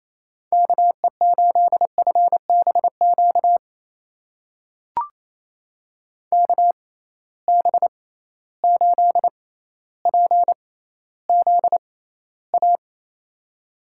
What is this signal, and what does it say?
Morse code, heard as a keyed tone, about 28 wpm, spaced as for 14 wpm, 700 Hz. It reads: KE8FBQ KB8PZA
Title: USA Call Signs - 6 Characters - 6x Character Spacing - 28wpm